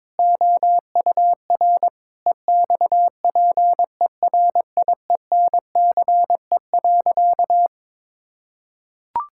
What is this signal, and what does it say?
Morse code, heard as a keyed tone, about 22 wpm, 700 Hz. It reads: OUR EXPERIENCE.
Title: Sentences from Top 1000 Words - Review Part 5 - 22wpm